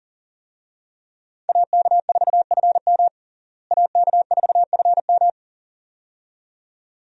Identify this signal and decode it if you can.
Morse code, heard as a keyed tone, about 40 wpm, 700 Hz. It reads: AK4FM AK4FM
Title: USA Call Signs - 5 Characters - Repeated 2x - 40wpm